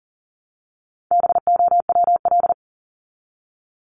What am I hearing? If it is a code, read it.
Morse code, heard as a keyed tone, about 40 wpm, 700 Hz. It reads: BOWL
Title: Top 1001-1250 Words 40wpm